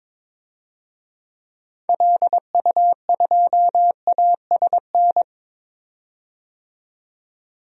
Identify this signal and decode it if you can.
Morse code, heard as a keyed tone, about 22 wpm, 700 Hz. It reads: LU2ASN
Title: DXCC-100 Call Signs - 6 Characters - 22wpm